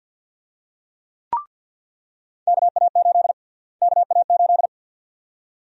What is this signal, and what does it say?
Morse code, heard as a keyed tone, about 50 wpm, 700 Hz. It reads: KA8 KA8
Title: USA Call Sign Prefixes - Repeated 2x - 50wpm